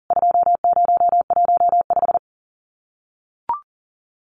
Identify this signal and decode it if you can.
Morse code, heard as a keyed tone, about 40 wpm, 700 Hz. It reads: 2015
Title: CWT Exchange 40wpm